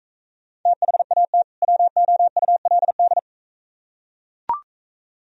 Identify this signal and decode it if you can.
Morse code, heard as a keyed tone, about 44 wpm, 700 Hz. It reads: THAT WOULD